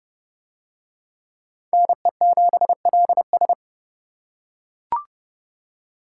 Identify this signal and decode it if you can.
Morse code, heard as a keyed tone, about 31 wpm, 700 Hz. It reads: NE7LS